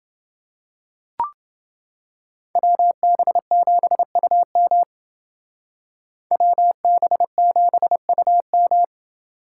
Morse code, heard as a keyed tone, about 31 wpm, 700 Hz. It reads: WB7UM WB7UM